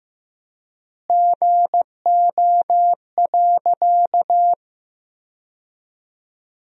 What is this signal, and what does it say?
Morse code, heard as a keyed tone, about 15 wpm, 700 Hz. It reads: GO.
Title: 3x Word-Spacing - Sentences from Top 100 Words - 15wpm